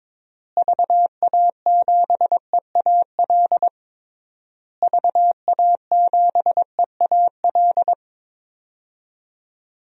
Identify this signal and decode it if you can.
Morse code, heard as a keyed tone, about 22 wpm, 700 Hz. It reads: VA7EAL VA7EAL
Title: DXCC-100 Call Signs - 6 Characters - Repeated 3x - 22wpm